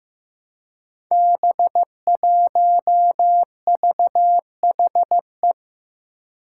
Morse code, heard as a keyed tone, about 15 wpm, 700 Hz. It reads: B1VHE
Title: Five Challenging Letter-Number Combinations 15wpm